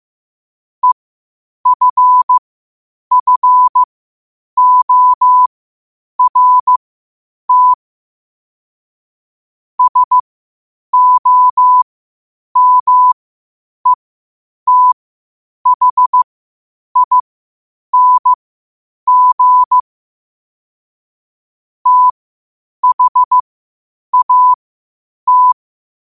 Morse code, heard as a keyed tone, about 15 wpm, spaced as for 8 wpm, 1000 Hz. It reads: EFFORT SOMETHING THAT
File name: qotdfiles / 8wpm